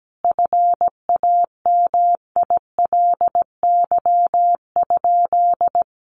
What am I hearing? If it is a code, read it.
Morse code, heard as a keyed tone, about 17 wpm, 700 Hz. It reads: FAMILY?
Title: Sentences from Top 1100 Words 17wpm